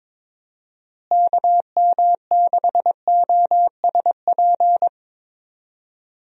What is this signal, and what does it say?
Morse code, heard as a keyed tone, about 22 wpm, 700 Hz. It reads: KM6OSP
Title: USA Call Signs - 6 Characters - Repeated 2x - 22wpm